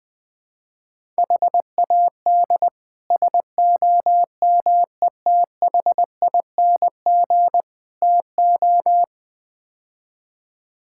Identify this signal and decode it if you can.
Morse code, heard as a keyed tone, about 20 wpm, 700 Hz. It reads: HAD SOMETHING TO
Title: Build A Phrase - 5-Word Phrases - 20wpm